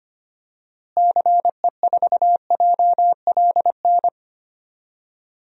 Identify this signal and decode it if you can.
Morse code, heard as a keyed tone, about 25 wpm, 700 Hz. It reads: CE4JLN
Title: DXCC-100 Call Signs - 6 Characters - Repeated 3x - 25wpm